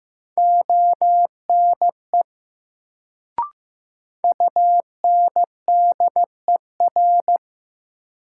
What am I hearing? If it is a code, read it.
Morse code, heard as a keyed tone, about 15 wpm, 700 Hz. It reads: ONE UNDER
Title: Sets of 2 Words - New Word Drill - 15wpm